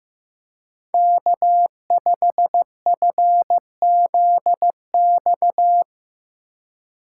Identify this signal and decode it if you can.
Morse code, heard as a keyed tone, about 17 wpm, 700 Hz. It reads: K5FZX
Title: Speed Racing - USA 1x3 Call Signs 15wpm